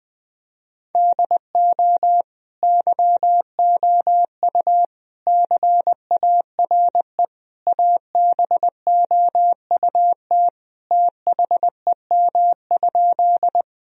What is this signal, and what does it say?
Morse code, heard as a keyed tone, about 19 wpm, 700 Hz. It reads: DO YOU CARE ABOUT THEM?
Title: Sentences from Top 1000 Words - Review Part 1 - 20wpm